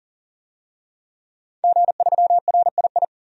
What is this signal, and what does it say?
Morse code, heard as a keyed tone, about 39 wpm, 700 Hz. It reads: G3RII
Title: DXCC-100 Call Signs - 5 Characters - 40wpm